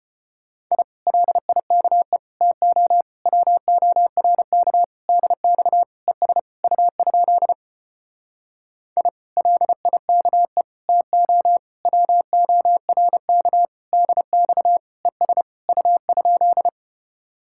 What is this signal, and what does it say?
Morse code, heard as a keyed tone, about 33 wpm, 700 Hz. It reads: I LIKE TO WORK DX ES U? I LIKE TO WORK DX ES U?